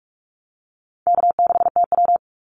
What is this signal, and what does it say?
Morse code, heard as a keyed tone, about 44 wpm, 700 Hz. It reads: K6TW
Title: NA SOTA Activator Callsigns 45wpm